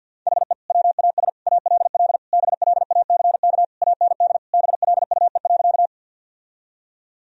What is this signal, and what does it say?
Morse code, heard as a keyed tone, about 52 wpm, 700 Hz. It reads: HE WAS ALL BLACK AND BLUE.
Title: Sentences from Top 1000 Words - Review Part 3 - 50wpm